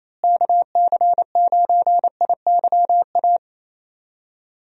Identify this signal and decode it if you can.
Morse code, heard as a keyed tone, about 28 wpm, 700 Hz. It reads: KC9IYA
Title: USA Call Signs - 6 Characters - Repeated 3x - 28wpm